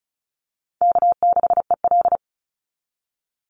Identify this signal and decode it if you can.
Morse code, heard as a keyed tone, about 34 wpm, 700 Hz. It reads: K6EL
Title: NA SOTA Chaser Callsigns 35wpm